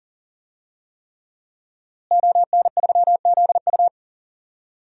Morse code, heard as a keyed tone, about 40 wpm, 700 Hz. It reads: ON3ZU
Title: DXCC-100 Call Signs - 5 Characters - Repeated 3x - 40wpm